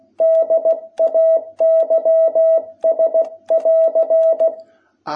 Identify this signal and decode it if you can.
Morse code, heard as a keyed tone, about 16 wpm, 660 Hz. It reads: DAYS<AR>
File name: jh8cbh / 2021cw